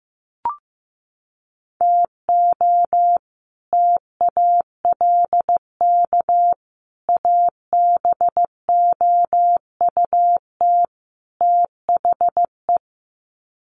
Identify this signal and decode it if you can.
Morse code, heard as a keyed tone, about 15 wpm, 700 Hz. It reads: TO TALK ABOUT THE
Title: Sets of 4 Words 15wpm